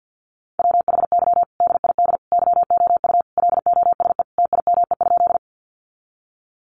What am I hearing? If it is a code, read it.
Morse code, heard as a keyed tone, about 48 wpm, 700 Hz. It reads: WHY DID YOU LOSE TIME?
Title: Sentences from Top 1100 Words 50wpm